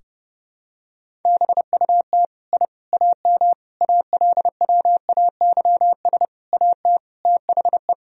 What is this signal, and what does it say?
Morse code, heard as a keyed tone, about 30 wpm, 700 Hz. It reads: BUT I AM ALWAYS AT THE
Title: Sentences from Top 1000 Words - Review Part 2 - 30wpm